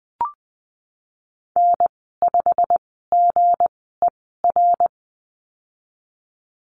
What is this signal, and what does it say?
Morse code, heard as a keyed tone, about 20 wpm, spaced as for 15 wpm, 700 Hz. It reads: N5GER
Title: 173 Lesson 28g - Number 9 - Call Signs 20wpm